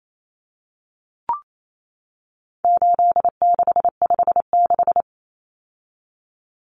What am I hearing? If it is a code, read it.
Morse code, heard as a keyed tone, about 28 wpm, 700 Hz. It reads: 8656
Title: Common Cut Numbers - 4-Digit 28wpm